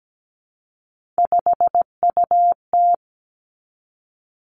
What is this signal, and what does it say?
Morse code, heard as a keyed tone, about 17 wpm, 700 Hz. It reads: 5UT
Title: Three Challenging Letter-Number Combinations 17wpm